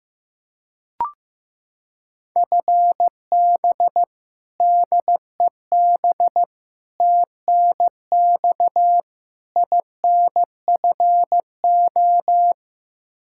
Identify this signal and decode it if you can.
Morse code, heard as a keyed tone, about 16 wpm, 700 Hz. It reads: FB DEB TNX INFO
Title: K1USN SST QSO Practice 15wpm